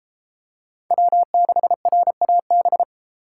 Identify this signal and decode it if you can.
Morse code, heard as a keyed tone, about 31 wpm, 700 Hz. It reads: W6RAB